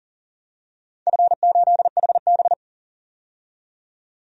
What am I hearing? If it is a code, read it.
Morse code, heard as a keyed tone, about 40 wpm, 700 Hz. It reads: F8HB
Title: DXCC-100 Call Signs - 4 Characters - 40wpm